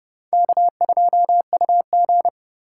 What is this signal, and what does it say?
Morse code, heard as a keyed tone, about 29 wpm, 700 Hz. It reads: K2UG